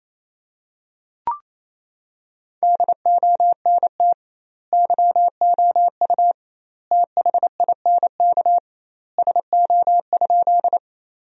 Morse code, heard as a keyed tone, about 28 wpm, 700 Hz. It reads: DONT YOU THINK SO?
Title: 2x Word-Spacing - Sentences from Top 300 Words - 28wpm